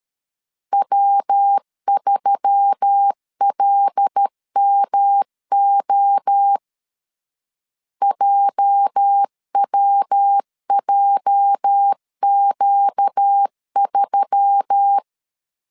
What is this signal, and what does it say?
Morse code, heard as a keyed tone, about 13 wpm, 800 Hz. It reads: W3LMO JWJQ3
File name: LEZIONI / 8 YJ45 / 8 YJ 12wpm 2235